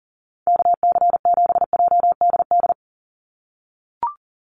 Morse code, heard as a keyed tone, about 40 wpm, 700 Hz. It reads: KC7JDD
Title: USA Call Signs 40wpm